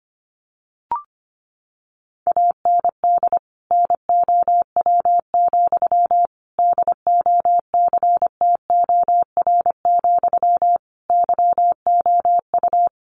Morse code, heard as a keyed tone, about 25 wpm, 700 Hz. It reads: AND NOW, DOCTOR, YOU
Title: Sentences from Top 1500 Words 25wpm